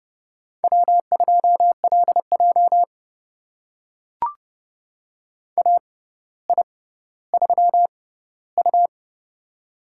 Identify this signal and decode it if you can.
Morse code, heard as a keyed tone, about 30 wpm, spaced as for 15 wpm, 700 Hz. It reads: W2LJ AI3U
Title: USA Call Signs - 4 Characters - 6x Character Spacing - 30wpm